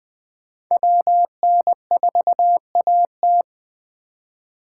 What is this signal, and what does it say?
Morse code, heard as a keyed tone, about 20 wpm, 700 Hz. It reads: WN4AT
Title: 2023 SOTA Chaser Call Signs - 20wpm